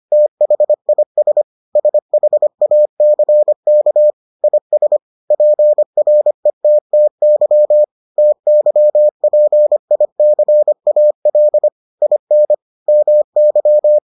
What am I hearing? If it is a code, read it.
Morse code, heard as a keyed tone, about 25 wpm, 600 Hz. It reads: THIS SHACK IS PRETTY TYPICAL IN MY